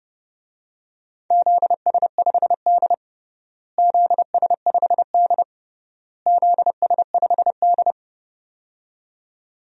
Morse code, heard as a keyed tone, about 30 wpm, 700 Hz. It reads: ZS5D ZS5D ZS5D
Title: DXCC-100 Call Signs - 4 Characters - Repeated 3x - 30wpm